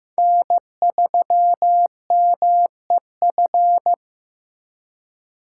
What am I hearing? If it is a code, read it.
Morse code, heard as a keyed tone, about 15 wpm, 700 Hz. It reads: N3MEF
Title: USA Call Signs - 5 Characters - 3x Character Spacing - 15wpm